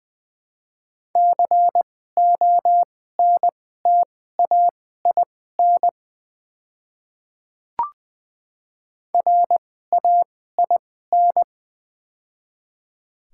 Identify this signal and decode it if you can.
Morse code, heard as a keyed tone, about 20 wpm, spaced as for 14 wpm, 700 Hz. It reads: CONTAIN RAIN
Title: 2x Character Spacing - Top 500 Words - 20wpm